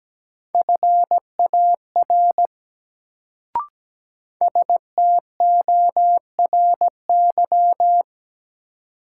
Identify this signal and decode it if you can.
Morse code, heard as a keyed tone, about 17 wpm, 700 Hz. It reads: FAR STORY
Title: Sets of 3 Words - New Word Drill - 17wpm